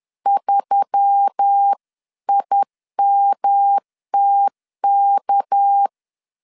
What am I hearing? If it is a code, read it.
Morse code, heard as a keyed tone, about 10 wpm, 800 Hz. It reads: 3IMTK
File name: LEZIONI / 6 PKX0 / 6 PK 10wpm 2436